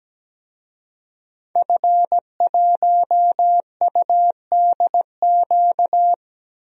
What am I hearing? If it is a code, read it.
Morse code, heard as a keyed tone, about 17 wpm, 700 Hz. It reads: F1UDQ
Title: DXCC-100 Call Signs - 5 Characters - 17wpm